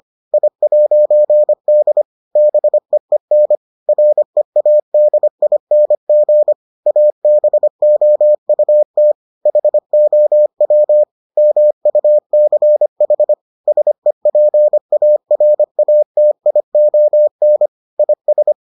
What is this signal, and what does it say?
Morse code, heard as a keyed tone, about 25 wpm, 600 Hz. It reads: I'D BEEN READING ABOUT HOW MUCH SEPARATION IS